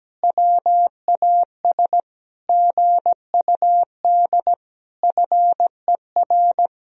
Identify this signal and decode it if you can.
Morse code, heard as a keyed tone, about 17 wpm, 700 Hz. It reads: WAS GUD FER
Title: Sentences from Top 400 Words and Abbreviations 17wpm